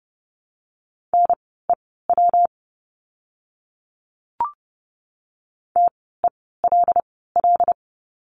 Morse code, heard as a keyed tone, about 30 wpm, spaced as for 17 wpm, 700 Hz. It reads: NEW TELL
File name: course / 30:10wpm / 237 Lesson 38f - Letter q - Words 30wpm